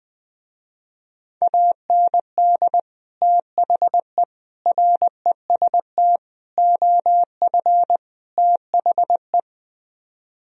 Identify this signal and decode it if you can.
Morse code, heard as a keyed tone, about 20 wpm, 700 Hz. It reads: AND THE REST OF THE
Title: Speed-Racing - Sets of 5 Words 17wpm